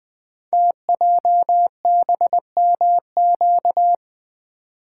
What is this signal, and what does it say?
Morse code, heard as a keyed tone, about 20 wpm, 700 Hz. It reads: TJBMQ
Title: Code Groups - 5 Letters - 20wpm